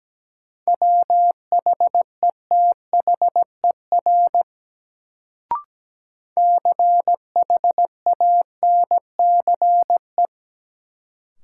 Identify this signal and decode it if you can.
Morse code, heard as a keyed tone, about 17 wpm, 700 Hz. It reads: WHETHER CHANCE
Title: Sets of 3 Words - New Word Drill - 17wpm